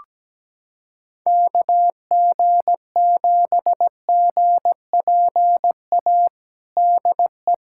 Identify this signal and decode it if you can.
Morse code, heard as a keyed tone, about 17 wpm, 700 Hz. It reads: KG7GPA DE
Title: USA Callsign DE Callsign 17wpm